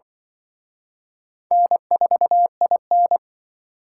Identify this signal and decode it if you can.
Morse code, heard as a keyed tone, about 24 wpm, 700 Hz. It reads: N4IN
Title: Speed Racing - USA 1x2 Call Signs 20wpm